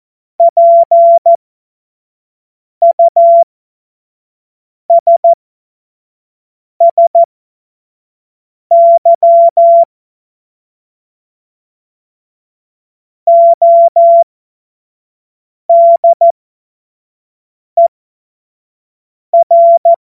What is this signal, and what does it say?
Morse code, heard as a keyed tone, about 14 wpm, spaced as for 5 wpm, 680 Hz. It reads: PUSSY ODER
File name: Morsekurs / Witze / Witz 12. Oktober 2021, 25 F70